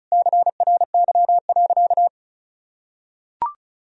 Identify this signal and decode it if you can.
Morse code, heard as a keyed tone, about 34 wpm, 700 Hz. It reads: CRY.
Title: Sentences from Top 1000 Words - Review Part 2 - 35wpm